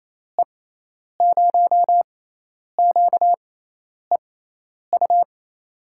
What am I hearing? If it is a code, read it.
Morse code, heard as a keyed tone, about 27 wpm, spaced as for 10 wpm, 700 Hz. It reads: E0QEU
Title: USA Call Signs - 6 Characters - 6x Character Spacing - 28wpm